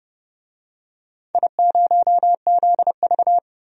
Morse code, heard as a keyed tone, about 29 wpm, 700 Hz. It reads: I0ZV
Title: DXCC-100 Call Signs - 4 Characters - 30wpm